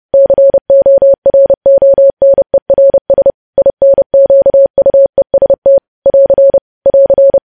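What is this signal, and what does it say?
Morse code, heard as a keyed tone, about 30 wpm, 550 Hz. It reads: CORONERS INQUEST <AR> <AR>
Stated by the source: Morse code news headlines sent at 30, preceded by vvv vvv.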